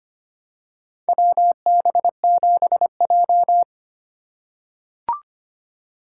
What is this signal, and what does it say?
Morse code, heard as a keyed tone, about 25 wpm, 700 Hz. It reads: WB7J
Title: USA Call Signs - 4 Characters - 3x Character Spacing - 25wpm